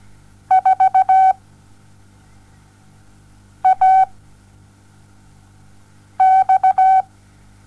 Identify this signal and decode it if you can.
Morse code, heard as a keyed tone, about 17 wpm, 760 Hz. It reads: 4 A X
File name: Onlinelehrgang / morsen / DJ4UF Morsen07